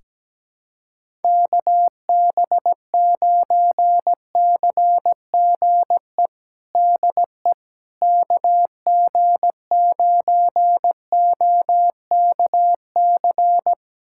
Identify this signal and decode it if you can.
Morse code, heard as a keyed tone, about 17 wpm, 700 Hz. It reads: KB9CGE DE KG9OKC
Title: USA Callsign DE Callsign - 6 Characters - 17wpm